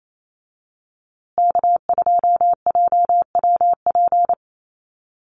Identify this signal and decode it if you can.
Morse code, heard as a keyed tone, about 28 wpm, 700 Hz. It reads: K2JWP
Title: USA Call Signs Repeated 3x 28wpm